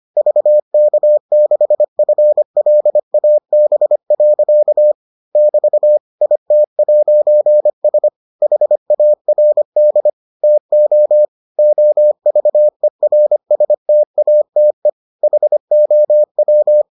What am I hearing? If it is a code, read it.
Morse code, heard as a keyed tone, about 25 wpm, 600 Hz. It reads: VK6FLAB. <BT> IT'S HARD TO OVERSTATE HOW